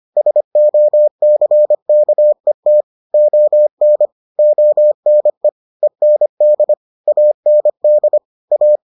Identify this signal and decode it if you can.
Morse code, heard as a keyed tone, about 25 wpm, 600 Hz. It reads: SOCKET ON ONE END AND A